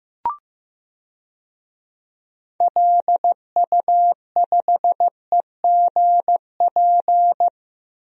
Morse code, heard as a keyed tone, about 15 wpm, 700 Hz. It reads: LU5EGP
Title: DXCC-100 Call Signs - 6 Characters - 15wpm